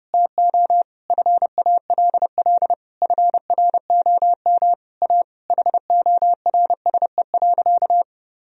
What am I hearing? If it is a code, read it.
Morse code, heard as a keyed tone, about 30 wpm, 700 Hz. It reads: TO FALL FROM A HORSE.
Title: Sentences from Top 1000 Words - Review Part 2 - 30wpm